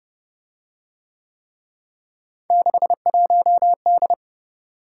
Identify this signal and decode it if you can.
Morse code, heard as a keyed tone, about 27 wpm, 700 Hz. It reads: 61D